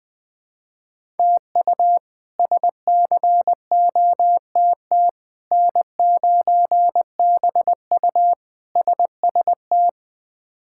Morse code, heard as a keyed tone, about 20 wpm, 700 Hz. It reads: TU SCOTT N9BU SST
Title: K1USN SST QSO Practice - 20wpm